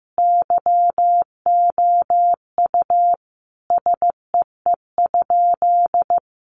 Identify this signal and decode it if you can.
Morse code, heard as a keyed tone, about 15 wpm, 700 Hz. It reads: YOU SEE?
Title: Sentences from Top 100 Words 15wpm